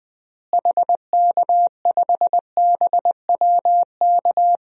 Morse code, heard as a keyed tone, about 20 wpm, 700 Hz. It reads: HK5BWK
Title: DXCC-100 Call Signs - 6 Characters - Repeated 3x - 20wpm